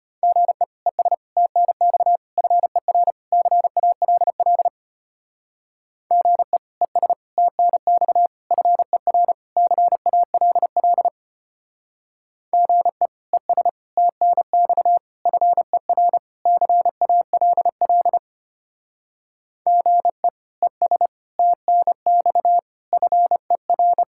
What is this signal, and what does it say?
Morse code, heard as a keyed tone, about 32 wpm, 700 Hz. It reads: GE ES TNX FER CALL GE ES TNX FER CALL GE ES TNX FER CALL GE ES TNX FER